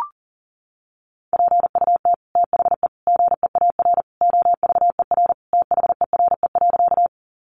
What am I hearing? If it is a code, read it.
Morse code, heard as a keyed tone, about 40 wpm, 700 Hz. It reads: PUT THE GEAR OVER THERE.
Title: Sentences from Top 1400 Words 40wpm